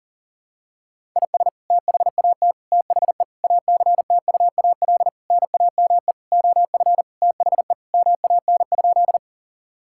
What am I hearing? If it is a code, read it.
Morse code, heard as a keyed tone, about 40 wpm, 700 Hz. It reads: IS THAT THE ACTUAL NAME OF THE MAN?
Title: Sentences from Top 1000 Words - Review Part 5 - 40wpm